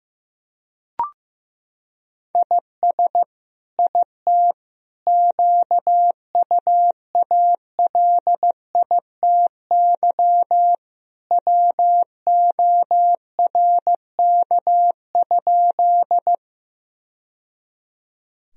Morse code, heard as a keyed tone, about 15 wpm, 700 Hz. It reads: IS IT QUALITY WORK?
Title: Sentences from Top 1000 Words - Review Part 4 - 15wpm